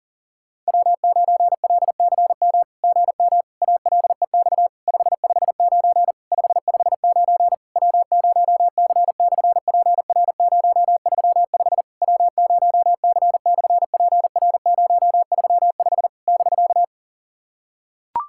Morse code, heard as a keyed tone, about 39 wpm, 700 Hz. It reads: W9LCM GM ALEX 559 559 W0C/PR035 W0C/PR035 <BK>
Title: SOTA QSO Practice 40wpm